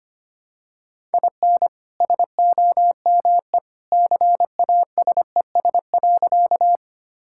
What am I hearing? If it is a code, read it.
Morse code, heard as a keyed tone, about 25 wpm, 700 Hz. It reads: IN SOME CASES.
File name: practice / 25wpm / Sentences with 2-Word Phrases - 25wpm